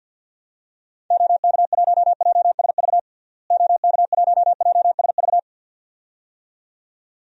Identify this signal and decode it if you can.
Morse code, heard as a keyed tone, about 51 wpm, 700 Hz. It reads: OK1JSV OK1JSV
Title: DXCC-100 Call Signs - 6 Characters - Repeated 2x - 50wpm